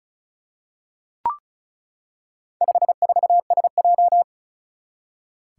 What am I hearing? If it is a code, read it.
Morse code, heard as a keyed tone, about 36 wpm, 700 Hz. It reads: 54SJ
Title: Four Challenging Letter-Number Combinations 35wpm